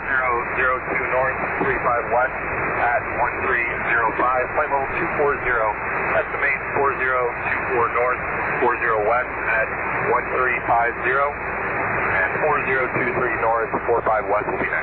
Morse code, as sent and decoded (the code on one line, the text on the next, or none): none